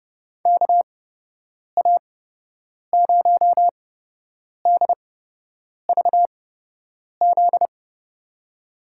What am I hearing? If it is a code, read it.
Morse code, heard as a keyed tone, about 30 wpm, spaced as for 8 wpm, 700 Hz. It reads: KA0DVZ
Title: USA Call Signs - 6 Characters - 8x Character Spacing - 30wpm